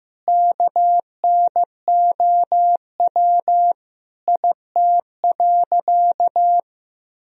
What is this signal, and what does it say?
Morse code, heard as a keyed tone, about 15 wpm, 700 Hz. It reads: KNOW IT.
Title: Sentences from Top 1000 Words - Review Part 2 - 15wpm